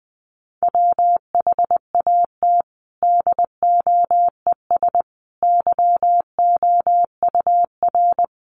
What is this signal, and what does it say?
Morse code, heard as a keyed tone, about 20 wpm, 700 Hz. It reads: WHAT DOES YOUR
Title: Sentences from Top 1400 Words 20wpm